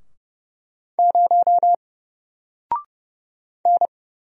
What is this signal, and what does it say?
Morse code, heard as a keyed tone, about 31 wpm, 700 Hz. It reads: 0 N
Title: Single Letter-Number 30wpm